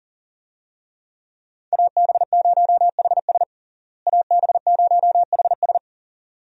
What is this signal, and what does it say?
Morse code, heard as a keyed tone, about 37 wpm, 700 Hz. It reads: AB0HS AB0HS
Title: USA Call Signs - 5 Characters - Repeated 2x - 40wpm